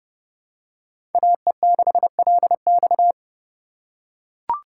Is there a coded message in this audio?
Yes: Morse code from 1.1 to 3.1 s, heard as a keyed tone, about 28 wpm, 700 Hz. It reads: AE6LX